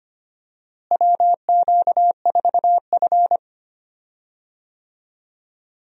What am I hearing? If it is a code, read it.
Morse code, heard as a keyed tone, about 25 wpm, 700 Hz. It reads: WQ4F
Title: USA Call Signs - 4 Characters - Repeated 2x - 25wpm